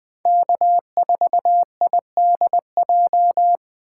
Morse code, heard as a keyed tone, about 20 wpm, 700 Hz. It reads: K4IDJ
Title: Speed Racing - USA 1x3 Call Signs 17wpm